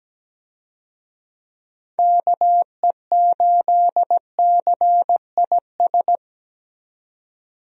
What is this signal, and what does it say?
Morse code, heard as a keyed tone, about 17 wpm, 700 Hz. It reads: KE8CIS
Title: USA Call Signs - 6 Characters - 17wpm